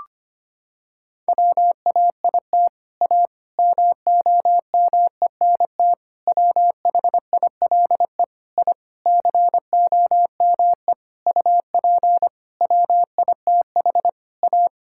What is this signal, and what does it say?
Morse code, heard as a keyed tone, about 25 wpm, 700 Hz. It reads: WAIT A MOMENT WHILE I COME UP WITH A
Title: Sentences from Top 1000 Words - Review Part 5 - 25wpm